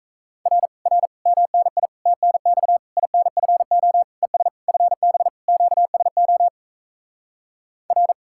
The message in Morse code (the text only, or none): R R MNI TNX INFO ES FB QSO R